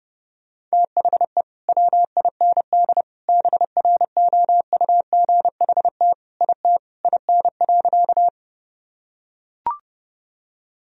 Morse code, heard as a keyed tone, about 30 wpm, 700 Hz. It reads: THE WIND BROUGHT IT IN.
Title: Sentences from Top 1000 Words - Review Part 2 - 30wpm